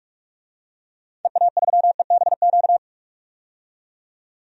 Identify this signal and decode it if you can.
Morse code, heard as a keyed tone, about 42 wpm, 700 Hz. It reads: EA3EBQ
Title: DXCC-100 Call Signs - 6 Characters - 45wpm